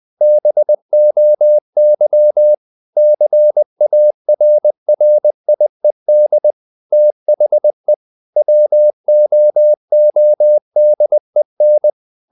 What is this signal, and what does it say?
Morse code, heard as a keyed tone, about 20 wpm, 600 Hz. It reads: BOY CARRIED THE WOODEN
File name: the old man and the sea / the old man and the sea part 1 20wpm